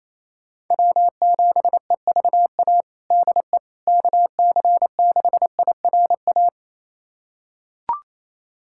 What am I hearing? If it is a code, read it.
Morse code, heard as a keyed tone, about 28 wpm, 700 Hz. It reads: W7EVA DE KC6IRA
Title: USA Callsign DE Callsign 28wpm